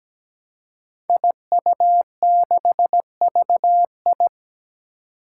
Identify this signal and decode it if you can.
Morse code, heard as a keyed tone, about 17 wpm, 700 Hz. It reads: IU6VI
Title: Five Challenging Letter-Number Combinations 17wpm